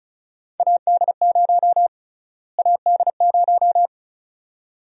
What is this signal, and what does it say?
Morse code, heard as a keyed tone, about 36 wpm, 700 Hz. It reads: AD0 AD0
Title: USA Call Sign Prefixes - Repeated 2x - 35wpm